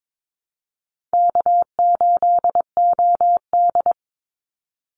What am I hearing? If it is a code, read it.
Morse code, heard as a keyed tone, about 22 wpm, 700 Hz. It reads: K8OD
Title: Easy Call Signs - 22wpm